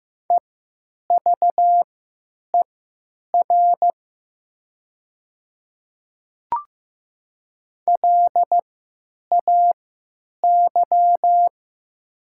Morse code, heard as a keyed tone, about 15 wpm, spaced as for 9 wpm, 700 Hz. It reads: EVER LAY
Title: 3x Character Spacing - Top 500 Words - 15wpm